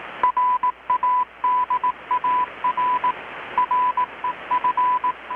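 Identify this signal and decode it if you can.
Morse code, heard as a keyed tone, about 18 wpm, 1000 Hz. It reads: RADAR REF
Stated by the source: Morse from KPH's Night of Nights XX maritime radiotelegraphy event, received in Chiapas, Mexico about 3,800 km away on a Kenwood R-600 with a grounded 10 m long-wire antenna.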